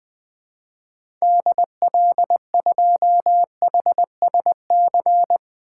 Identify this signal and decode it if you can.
Morse code, heard as a keyed tone, about 20 wpm, 700 Hz. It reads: DL2HSC
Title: 2023 SOTA Activator Call Signs - 20wpm